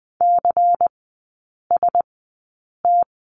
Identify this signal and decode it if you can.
Morse code, heard as a keyed tone, about 20 wpm, spaced as for 8 wpm, 700 Hz. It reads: CST
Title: 107 Lesson 18e - Letter m - Three Characters 20wpm